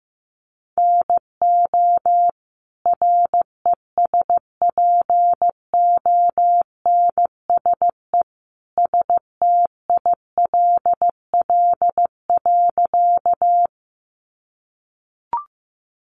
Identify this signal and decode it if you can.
Morse code, heard as a keyed tone, about 14 wpm, 700 Hz. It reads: NO RESPONSE STILL.
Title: Sentences from Top 1400 Words 15wpm